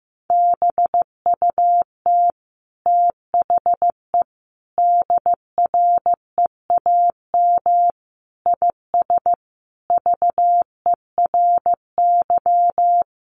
Morse code, heard as a keyed tone, about 15 wpm, 700 Hz. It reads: BUT THE DREAM IS VERY
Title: Sentences from Top 1500 Words 15wpm